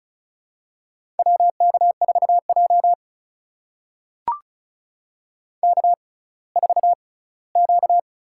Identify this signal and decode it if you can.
Morse code, heard as a keyed tone, about 36 wpm, spaced as for 20 wpm, 700 Hz. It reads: WK4J K4Q
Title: USA Call Signs - 4 Characters - 6x Character Spacing - 35wpm